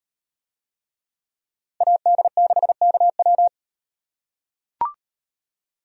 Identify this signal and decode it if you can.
Morse code, heard as a keyed tone, about 38 wpm, 700 Hz. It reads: AD6KW